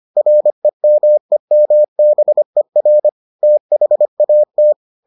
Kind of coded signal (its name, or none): Morse code